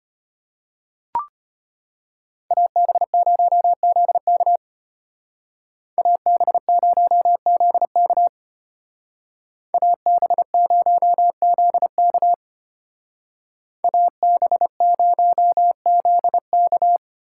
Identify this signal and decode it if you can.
Morse code, heard as a keyed tone, about 29 wpm, 700 Hz. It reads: AB0ZK AB0ZK AB0ZK AB0ZK
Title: Speed Racing - USA 2x2 Call Signs 25wpm